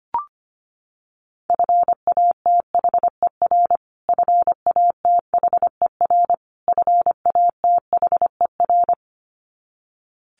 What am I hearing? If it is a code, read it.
Morse code, heard as a keyed tone, about 25 wpm, 700 Hz. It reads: FATHER FATHER FATHER
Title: Top 200 Words Repeated 25wpm